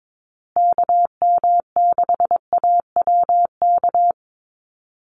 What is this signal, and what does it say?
Morse code, heard as a keyed tone, about 22 wpm, 700 Hz. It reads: KM6AWK
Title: USA Call Signs Repeated 2x 22wpm